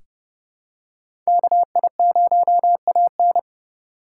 Morse code, heard as a keyed tone, about 30 wpm, 700 Hz. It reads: KI0AN
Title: USA Call Signs - 2x Character Spacing - 30wpm